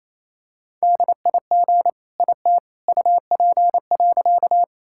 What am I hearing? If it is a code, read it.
Morse code, heard as a keyed tone, about 28 wpm, 700 Hz. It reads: DIG IT UP.
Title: Sentences from Top 1600 Words 28wpm